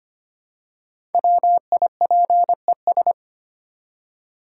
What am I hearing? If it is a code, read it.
Morse code, heard as a keyed tone, about 25 wpm, 690 Hz. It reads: WIPES
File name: practice / 25wpm / Easy Words with Hints - 25wpm